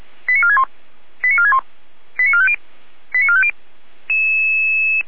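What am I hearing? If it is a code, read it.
Five-tone selective call (ZVEI), heard as quick runs of five tones, tones of about 70 ms. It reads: 89461 89461 89450 89450
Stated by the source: ZVEI five-tone selective call received on SDR.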